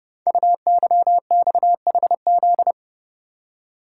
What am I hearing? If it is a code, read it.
Morse code, heard as a keyed tone, about 30 wpm, 700 Hz. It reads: UYXHZ
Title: Code Groups - 5 Letters - 30wpm